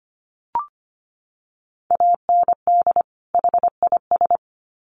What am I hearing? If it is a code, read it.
Morse code, heard as a keyed tone, about 25 wpm, 700 Hz. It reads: AND HIS
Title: Sets of 2 Words 25wpm